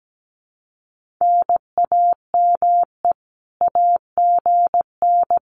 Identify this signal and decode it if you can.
Morse code, heard as a keyed tone, about 17 wpm, 700 Hz. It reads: NAME AGN
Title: Common QSO Phrases Encore 17wpm